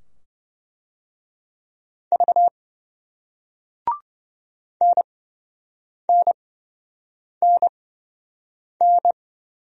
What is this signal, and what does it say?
Morse code, heard as a keyed tone, about 26 wpm, spaced as for 7 wpm, 700 Hz. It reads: V NNNN